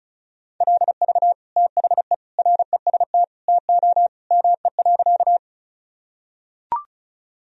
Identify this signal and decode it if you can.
Morse code, heard as a keyed tone, about 34 wpm, 700 Hz. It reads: LV THE REST TO ME.
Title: Sentences from Top 500 Words and Abbreviations 35wpm